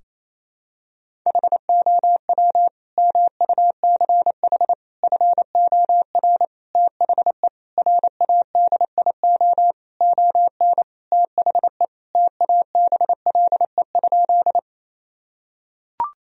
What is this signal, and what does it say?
Morse code, heard as a keyed tone, about 28 wpm, 700 Hz. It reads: HOW MUCH FOR THE RADIO ON THE TABLE?
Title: Sentences from Top 1000 Words - Review Part 3 - 28wpm